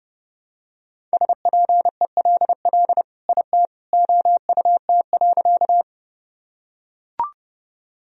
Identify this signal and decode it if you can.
Morse code, heard as a keyed tone, about 30 wpm, 700 Hz. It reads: SPELL IT OUT.
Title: Sentences from Top 1000 Words - Review Part 1 - 30wpm